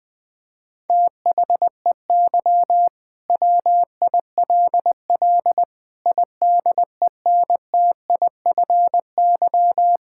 Morse code, heard as a keyed tone, about 20 wpm, 700 Hz. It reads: THEY WILL IDENTIFY
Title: Sentences from Top 1000 Words - Review Part 6 - 20wpm